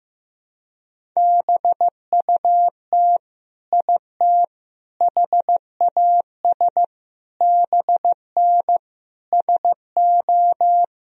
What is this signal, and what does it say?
Morse code, heard as a keyed tone, about 15 wpm, 700 Hz. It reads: BUT IT HAS BN SO
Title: Sentences from Top 100 Words and Abbreviations 15wpm